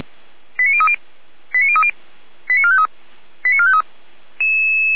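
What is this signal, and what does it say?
Five-tone selective call (ZVEI), heard as quick runs of five tones, tones of about 70 ms. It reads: 89030 89030 89453 89453
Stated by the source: ZVEI five-tone selective call received on SDR.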